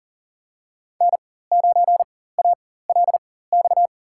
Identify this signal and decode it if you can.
Morse code, heard as a keyed tone, about 40 wpm, spaced as for 19 wpm, 700 Hz. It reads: N9ALX
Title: USA Call Signs - 5 Characters - 4x Character Spacing - 40wpm